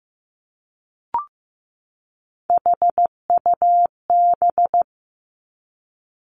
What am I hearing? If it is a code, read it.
Morse code, heard as a keyed tone, about 15 wpm, 700 Hz. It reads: HUB
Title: Three Challenging Letter-Number Combinations 15wpm